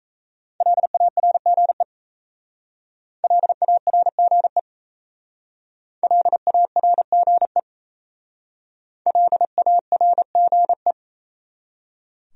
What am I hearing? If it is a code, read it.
Morse code, heard as a keyed tone, about 37 wpm, 700 Hz. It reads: LARGE LARGE LARGE LARGE